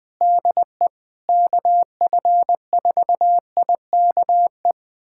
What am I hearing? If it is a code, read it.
Morse code, heard as a keyed tone, about 21 wpm, 700 Hz. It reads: DE KF4IKE
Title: USA Callsign DE Callsign 20wpm